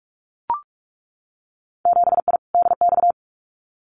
Morse code, heard as a keyed tone, about 44 wpm, 700 Hz. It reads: 7I DX
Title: Winter Field Day Exchanges Encore 45wpm